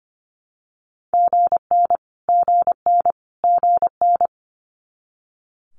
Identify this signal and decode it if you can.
Morse code, heard as a keyed tone, about 25 wpm, 700 Hz. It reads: GN GN GN
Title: Multi-letter Phonemes Repeated 3x 25wpm